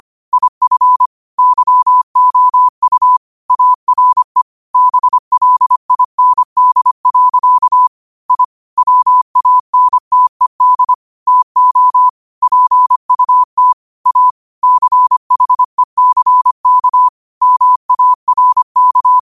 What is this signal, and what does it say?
Morse code, heard as a keyed tone, about 25 wpm, 1000 Hz. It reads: IF YOU ARE BLIND. I WANTED TO PUT A CHECK MARK
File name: qotdfiles / 25wpm